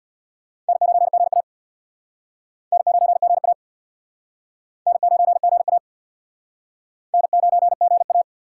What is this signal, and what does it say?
Morse code, heard as a keyed tone, about 62 wpm, 700 Hz. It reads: N9GA N9GA N9GA N9GA
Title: Speed Racing - USA 1x2 Call Signs 50wpm